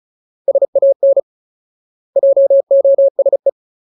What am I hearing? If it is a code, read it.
Morse code, heard as a keyed tone, about 34 wpm, 550 Hz. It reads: SAN JOSE
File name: Ultimate Word List / 35WPM Ultimate